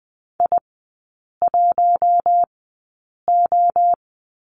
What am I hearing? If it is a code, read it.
Morse code, heard as a keyed tone, about 20 wpm, spaced as for 8 wpm, 700 Hz. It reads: I1O
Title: 041 Lesson 8e - Number 1 - Three Characters 20wpm